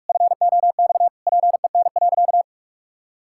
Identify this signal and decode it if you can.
Morse code, heard as a keyed tone, about 48 wpm, 700 Hz. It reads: FOX PEN.
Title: Pangram Challenge v2 - 45wpm